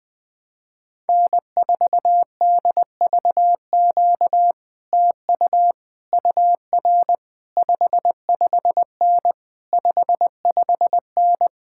Morse code, heard as a keyed tone, about 20 wpm, 700 Hz. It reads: N4DVQ TU UR 55N 55N
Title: Speed-Racing Common QSO Phrases - 17wpm